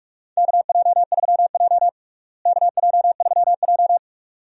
Morse code, heard as a keyed tone, about 46 wpm, 700 Hz. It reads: KJ3J KJ3J
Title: USA Call Signs - 4 Characters - Repeated 3x - 45wpm